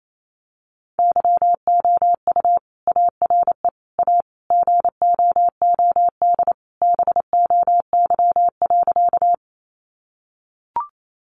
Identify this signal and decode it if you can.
Morse code, heard as a keyed tone, about 28 wpm, 700 Hz. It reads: YOU ARE A GOOD BOY.
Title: Sentences from Top 200 Words 28wpm